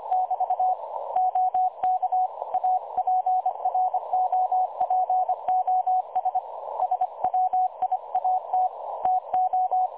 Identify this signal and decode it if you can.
Morse code, heard as a keyed tone, about 25 wpm, 740 Hz. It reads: <BT> OK APROPOS SWIAT TO
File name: dlazaawansowanych / nagraniazpasma / websdr recording 3565.0kHz